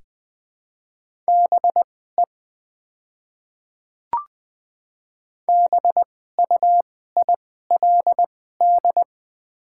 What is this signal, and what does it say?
Morse code, heard as a keyed tone, about 20 wpm, spaced as for 14 wpm, 700 Hz. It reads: BE BUILD
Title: 2x Character Spacing - Top 200 Words - 20wpm